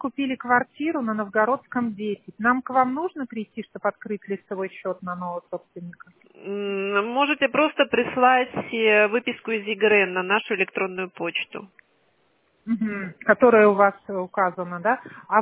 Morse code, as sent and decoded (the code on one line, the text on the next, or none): none